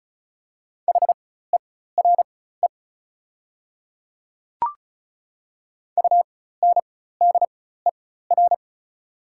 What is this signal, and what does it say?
Morse code, heard as a keyed tone, about 36 wpm, spaced as for 16 wpm, 700 Hz. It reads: HERE UNDER
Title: 4x Character Spacing - Top 200 Words - 35wpm